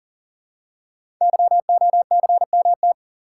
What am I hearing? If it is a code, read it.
Morse code, heard as a keyed tone, about 40 wpm, 700 Hz. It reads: YOCMT
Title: Code Groups - 5 Letters - 40wpm